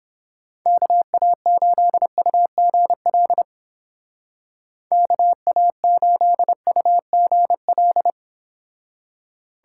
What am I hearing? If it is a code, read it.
Morse code, heard as a keyed tone, about 29 wpm, 700 Hz. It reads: KA8UGL KA8UGL